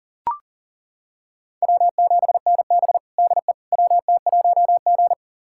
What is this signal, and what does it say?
Morse code, heard as a keyed tone, about 37 wpm, 700 Hz. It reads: W7NB DE WT1G
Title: USA Callsign DE Callsign - 4 Characters - 40wpm